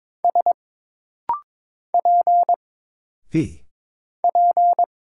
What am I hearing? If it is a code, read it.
Morse code, heard as a keyed tone, about 22 wpm, 700 Hz. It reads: S P P
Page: This rapid-fire timing is geared to push your ICR (Instant Character Recognition) skills.